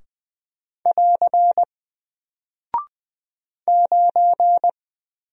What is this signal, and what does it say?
Morse code, heard as a keyed tone, about 20 wpm, 700 Hz. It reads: <AR> 9
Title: Single Character (Letter:Number:Punctuation:Prosign) Alt 20wpm